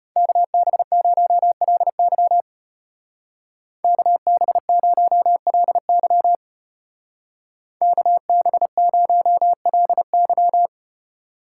Morse code, heard as a keyed tone, about 32 wpm, 700 Hz. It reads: KB0LY KB0LY KB0LY